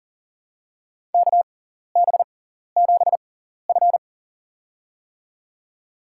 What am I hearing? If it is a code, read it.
Morse code, heard as a keyed tone, about 40 wpm, spaced as for 14 wpm, 700 Hz. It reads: KB7F
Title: USA Call Signs - 6x Character Spacing - 40wpm